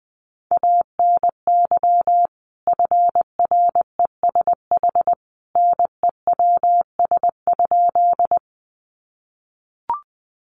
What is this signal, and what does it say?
Morse code, heard as a keyed tone, about 20 wpm, 700 Hz. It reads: ANY FRESH NEWS?
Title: Sentences from Top 1200 Words 20wpm